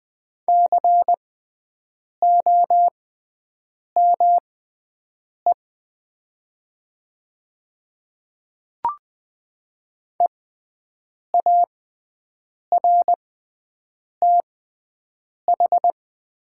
Morse code, heard as a keyed tone, about 20 wpm, spaced as for 7 wpm, 700 Hz. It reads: COME EARTH
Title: 6x Character Spacing - Top 200 Words - 20wpm